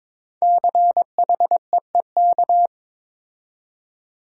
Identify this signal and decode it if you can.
Morse code, heard as a keyed tone, about 22 wpm, 700 Hz. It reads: CHEEK
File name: 5 Letter Words - First 3 Letters Given - 22wpm